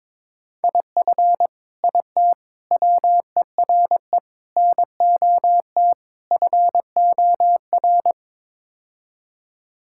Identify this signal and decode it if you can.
Morse code, heard as a keyed tone, about 21 wpm, 700 Hz. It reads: IF IT WERE NOT FOR
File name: practice / 22wpm / Build A Phrase - 5-Word Phrases - 22wpm